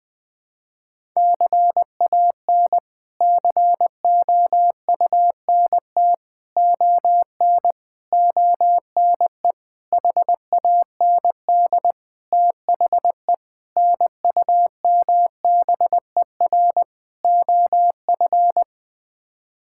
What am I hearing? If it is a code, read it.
Morse code, heard as a keyed tone, about 20 wpm, 700 Hz. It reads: CAN COUNT ON ONE HAND THE NUMBER OF
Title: Sets of 8 Words - 20wpm